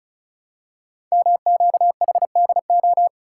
Morse code, heard as a keyed tone, about 36 wpm, 700 Hz. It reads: MQHDO
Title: Code Groups - 5 Letters - 35wpm